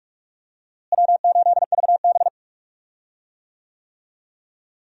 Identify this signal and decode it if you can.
Morse code, heard as a keyed tone, about 42 wpm, 700 Hz. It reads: W8VB
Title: USA Call Signs - 4 Characters - Repeated 3x - 45wpm